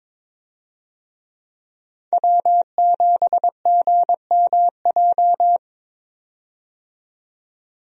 Morse code, heard as a keyed tone, about 22 wpm, 700 Hz. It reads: W7GMJ
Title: USA Call Signs - 5 Characters - 22wpm